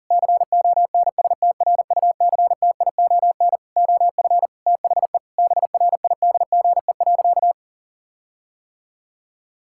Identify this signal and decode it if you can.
Morse code, heard as a keyed tone, about 37 wpm, 700 Hz. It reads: CONSTRUCTION OF THE BRIDGE.
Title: Sentences from Top 1600 Words 40wpm